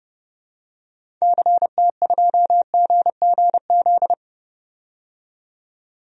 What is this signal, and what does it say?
Morse code, heard as a keyed tone, about 31 wpm, 700 Hz. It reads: CT2GGZ
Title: DXCC-100 Call Signs - 6 Characters - 30wpm